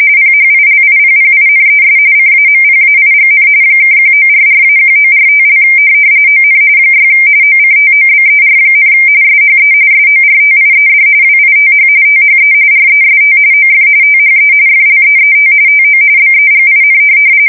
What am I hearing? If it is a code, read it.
RTTY, about 45.45 baud. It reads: PHILIPPINES CAMPAIGN (1944-1945) ON 6 OCTOBER, SHE SORTIED WITH THE REST OF THE FLEET TO BEGIN
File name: fsk170 / USS-Massachusetts